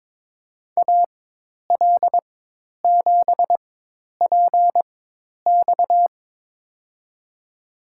Morse code, heard as a keyed tone, about 22 wpm, spaced as for 10 wpm, 700 Hz. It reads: AL7PX
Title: USA Call Signs - 5 Characters - 4x Character Spacing - 22wpm